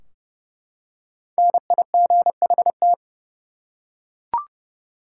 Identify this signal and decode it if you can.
Morse code, heard as a keyed tone, about 29 wpm, 700 Hz. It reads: NIGHT
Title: Five Letter Words - 30wpm